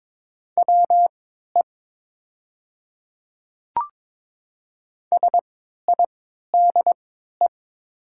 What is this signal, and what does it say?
Morse code, heard as a keyed tone, about 22 wpm, spaced as for 12 wpm, 700 Hz. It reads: WE SIDE
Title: 3x Character Spacing - Top 100 Words - 22wpm